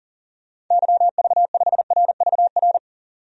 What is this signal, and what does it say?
Morse code, heard as a keyed tone, about 40 wpm, 700 Hz. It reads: YV5RVR
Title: DXCC-100 Call Signs - 6 Characters - 40wpm